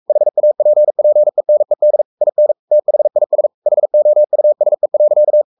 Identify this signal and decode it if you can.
Morse code, heard as a keyed tone, about 44 wpm, 600 Hz. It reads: HAPPENED IN THIS HOUSE.
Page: Romanen Never af Ken Follet på engelsk.